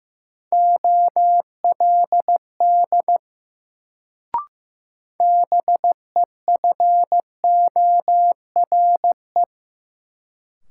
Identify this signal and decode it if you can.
Morse code, heard as a keyed tone, about 15 wpm, 700 Hz. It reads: OLD BEFORE
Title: Sets of 2 Words - New Word Drill - 15wpm